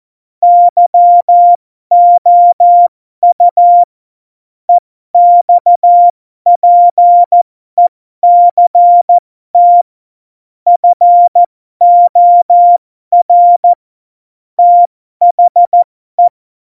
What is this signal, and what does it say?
Morse code, heard as a keyed tone, about 14 wpm, spaced as for 12 wpm, 700 Hz. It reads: YOU EXPECT FOR THE
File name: Morsekurs / Witze